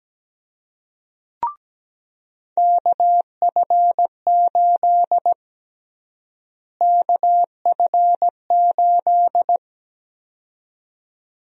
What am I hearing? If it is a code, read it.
Morse code, heard as a keyed tone, about 17 wpm, 700 Hz. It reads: KF8 KF8
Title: USA Call Sign Prefixes - Repeated 2x - 17wpm